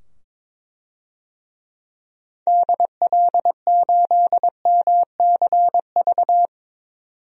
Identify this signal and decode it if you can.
Morse code, heard as a keyed tone, about 22 wpm, 700 Hz. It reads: DL8MCV
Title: DXCC-100 Call Signs - 6 Characters - 22wpm